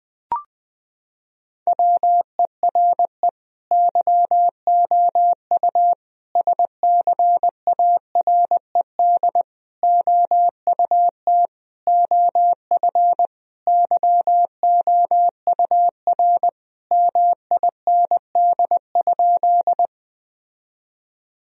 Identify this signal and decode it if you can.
Morse code, heard as a keyed tone, about 20 wpm, 700 Hz. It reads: WERE YOU SCARED OUT OF YOUR MIND?
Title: Sentences from Top 1000 Words - Review Part 4 - 20wpm